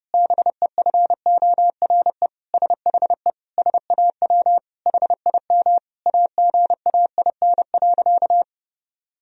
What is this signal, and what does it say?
Morse code, heard as a keyed tone, about 30 wpm, 700 Hz. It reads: BEFORE SHE SAW HIM AGAIN.
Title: Sentences from Top 1000 Words - Review Part 1 - 30wpm